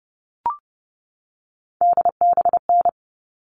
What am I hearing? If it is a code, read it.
Morse code, heard as a keyed tone, about 30 wpm, 700 Hz. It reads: DBN
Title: All Cut Numbers - 3-Digit 30wpm